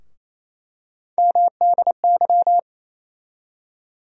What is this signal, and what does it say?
Morse code, heard as a keyed tone, about 28 wpm, 700 Hz. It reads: MDY
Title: USA Call Sign Postfixes - 3 Letters 28wpm